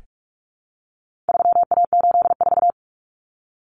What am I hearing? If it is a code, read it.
Morse code, heard as a keyed tone, about 44 wpm, 700 Hz. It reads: 3A84
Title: Common Cut Numbers - 4-Digit 45wpm